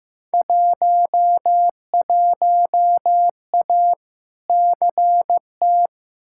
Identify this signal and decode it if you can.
Morse code, heard as a keyed tone, about 15 wpm, 700 Hz. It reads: 11A CT
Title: Field Day Exchanges Encore - 15wpm